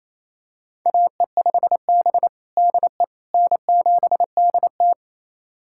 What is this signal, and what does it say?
Morse code, heard as a keyed tone, about 28 wpm, 700 Hz. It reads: AE5B DE N7DT
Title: USA Callsign DE Callsign - 4 Characters - 28wpm